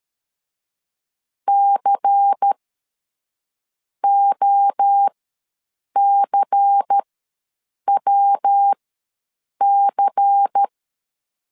Morse code, heard as a keyed tone, about 13 wpm, spaced as for 6 wpm, 800 Hz. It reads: COCWC
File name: LEZIONI / 5 LBVC / 5 LBVC ws 12wpm 3159